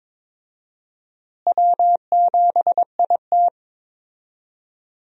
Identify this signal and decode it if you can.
Morse code, heard as a keyed tone, about 22 wpm, 700 Hz. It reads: W7IT
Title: USA Call Signs - 4 Characters - Repeated 3x - 22wpm